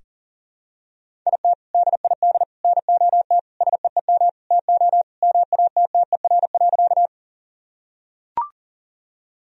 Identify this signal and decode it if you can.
Morse code, heard as a keyed tone, about 40 wpm, 700 Hz. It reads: IT DID NOT SEEM TO MATTER.
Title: Sentences from Top 1000 Words - Review Part 1 - 40wpm